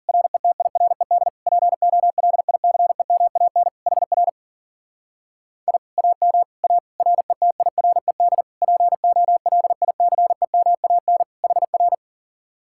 Morse code, heard as a keyed tone, about 51 wpm, 700 Hz. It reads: RETIRED POLICEMAN HR I AM A RETIRED POLICEMAN HR